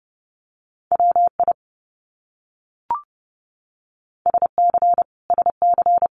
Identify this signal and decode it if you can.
Morse code, heard as a keyed tone, about 30 wpm, 700 Hz. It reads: WI SC SC
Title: US States Repeated 3x 30wpm